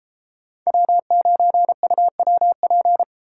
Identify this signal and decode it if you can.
Morse code, heard as a keyed tone, about 31 wpm, 700 Hz. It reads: W9UWP